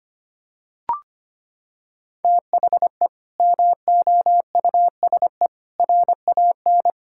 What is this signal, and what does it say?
Morse code, heard as a keyed tone, about 25 wpm, 700 Hz. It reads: THE MOUSE RAN
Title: Sentences from Top 1000 Words - Review Part 5 - 25wpm